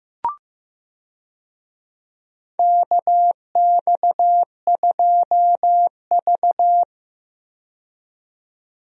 Morse code, heard as a keyed tone, about 15 wpm, 700 Hz. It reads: KX2V
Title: USA Call Signs - 4 Characters - 15wpm